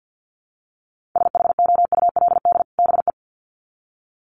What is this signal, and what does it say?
Morse code, heard as a keyed tone, about 48 wpm, 700 Hz. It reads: SHOULD BE
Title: Sets of 2 Words 50wpm